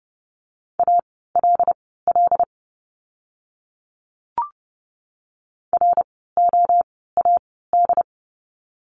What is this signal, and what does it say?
Morse code, heard as a keyed tone, about 30 wpm, spaced as for 17 wpm, 700 Hz. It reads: ALL ROAD
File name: Lesson 17f - Letter u - Words 30wpm